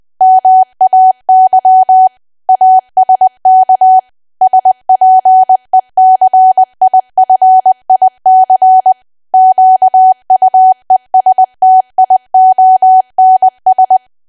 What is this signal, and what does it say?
Morse code, heard as a keyed tone, about 20 wpm, 750 Hz. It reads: MAY ASK SPECIFIC QUESTIONS
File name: cwa / W1AW-Level3 / 20WPM